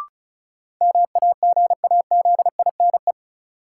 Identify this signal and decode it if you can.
Morse code, heard as a keyed tone, about 36 wpm, 700 Hz. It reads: MAGAZINE
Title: Sentences from Top 1000 Words - Review Part 7 - New Word Drill - 35wpm